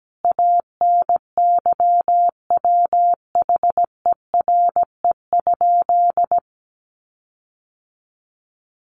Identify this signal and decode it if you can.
Morse code, heard as a keyed tone, about 17 wpm, 700 Hz. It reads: ANYWHERE?
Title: Sentences from Top 1400 Words 17wpm